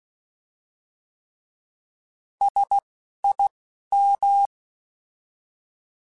Morse code, heard as a keyed tone, about 16 wpm, 800 Hz. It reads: S I M
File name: S vs H / S vs H mp3s / S vs H @10